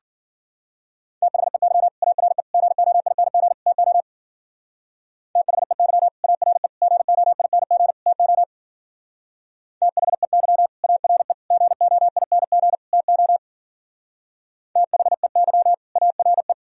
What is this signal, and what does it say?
Morse code, heard as a keyed tone, about 51 wpm, 700 Hz. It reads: THEY ARE GOING TO THEY ARE GOING TO THEY ARE GOING TO THEY ARE